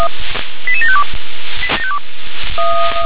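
Five-tone selective call (ZVEI), heard as quick runs of five tones, tones of about 70 ms. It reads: 88752
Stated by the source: ZVEI five-tone selective call 88712, received by SDR.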